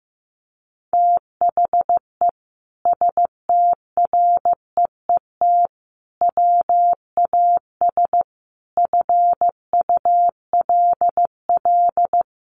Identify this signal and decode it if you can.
Morse code, heard as a keyed tone, about 14 wpm, 700 Hz. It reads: THE STREET WAS FULL
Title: Sentences from Top 1300 Words 15wpm